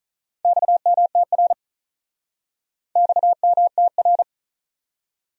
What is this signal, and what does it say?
Morse code, heard as a keyed tone, about 46 wpm, 700 Hz. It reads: XMTR XMTR